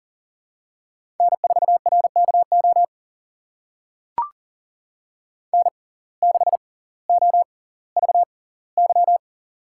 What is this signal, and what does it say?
Morse code, heard as a keyed tone, about 40 wpm, spaced as for 20 wpm, 700 Hz. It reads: N4RKO N6OVY
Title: USA Call Signs - 5 Characters - 6x Character Spacing - 40wpm